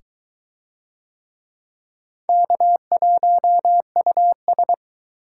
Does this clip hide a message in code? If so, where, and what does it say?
Yes: Morse code from 2.3 to 4.7 s, heard as a keyed tone, about 20 wpm, 700 Hz. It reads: K1US